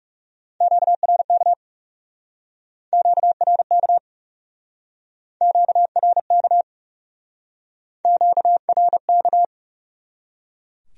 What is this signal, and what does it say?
Morse code, heard as a keyed tone, about 38 wpm, 700 Hz. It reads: QRK QRK QRK QRK